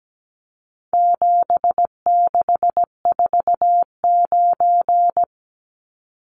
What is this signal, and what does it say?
Morse code, heard as a keyed tone, about 17 wpm, 700 Hz. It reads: 7649
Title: Numbers - 4-Digit 17wpm